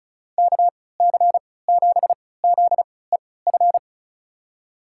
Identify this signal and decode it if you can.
Morse code, heard as a keyed tone, about 34 wpm, spaced as for 20 wpm, 700 Hz. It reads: KC7ZEF
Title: USA Call Signs - 6 Characters - 3x Character Spacing - 35wpm